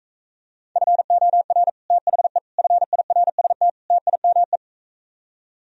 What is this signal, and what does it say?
Morse code, heard as a keyed tone, about 45 wpm, 700 Hz. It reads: FOR THE FIRST TIME